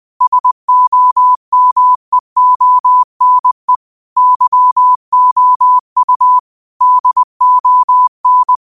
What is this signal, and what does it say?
Morse code, heard as a keyed tone, about 20 wpm, 1000 Hz. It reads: SOMEONE YOU DON